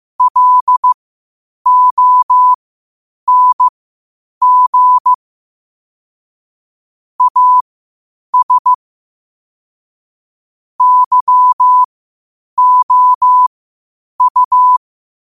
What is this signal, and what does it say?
Morse code, heard as a keyed tone, about 15 wpm, spaced as for 8 wpm, 1000 Hz. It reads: LONG AS YOU
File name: qotdfiles / 8wpm